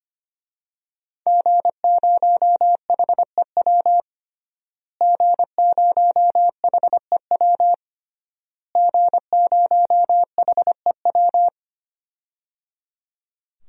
Morse code, heard as a keyed tone, about 25 wpm, 700 Hz. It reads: G0HEW G0HEW G0HEW
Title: DXCC-100 Call Signs - 5 Characters - Repeated 3x - 25wpm